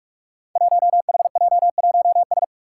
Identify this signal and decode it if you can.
Morse code, heard as a keyed tone, about 44 wpm, 700 Hz. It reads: 1HJ1S
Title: Five Challenging Letter-Number Combinations 45wpm